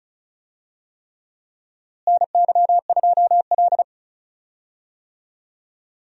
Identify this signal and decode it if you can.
Morse code, heard as a keyed tone, about 34 wpm, 700 Hz. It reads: NY2L
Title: USA Call Signs - 4 Characters - Repeated 3x - 35wpm